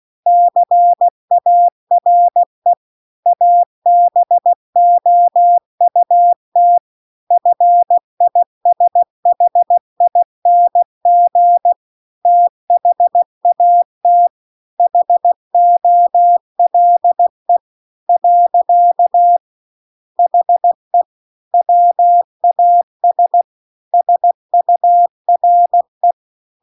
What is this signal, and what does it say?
Morse code, heard as a keyed tone, about 16 wpm, 700 Hz. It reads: CARE ABOUT FISHING THAT HOLE . HE WAS SURE